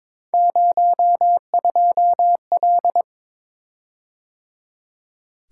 Morse code, heard as a keyed tone, about 22 wpm, 700 Hz. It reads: 02L
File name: practice / 22wpm / Three Letter-Number Combinations - 22wpm